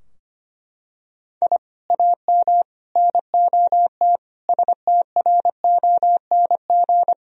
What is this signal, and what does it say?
Morse code, heard as a keyed tone, about 25 wpm, 700 Hz. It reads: I AM NOT STRONG
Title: Sentences from Top 1000 Words - Review Part 3 - 25wpm